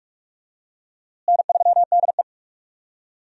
Morse code, heard as a keyed tone, about 42 wpm, 700 Hz. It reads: N3DE
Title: USA Call Signs - 4 Characters - Repeated 3x - 45wpm